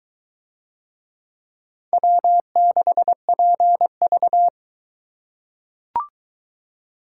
Morse code, heard as a keyed tone, about 21 wpm, 700 Hz. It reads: W6PV